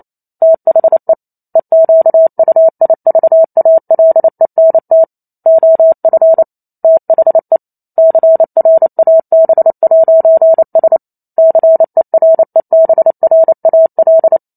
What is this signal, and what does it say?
Morse code, heard as a keyed tone, about 28 wpm, 650 Hz. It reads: THE EQUIVALENT OF THE CRAB'S CEREBRAL